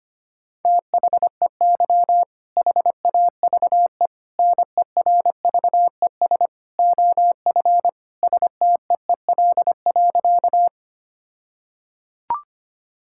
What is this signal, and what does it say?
Morse code, heard as a keyed tone, about 25 wpm, 700 Hz. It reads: THEY HAVE NERVES OF STEEL.
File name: practice / 25wpm / All-Age Appropriate Jokes - 25wpm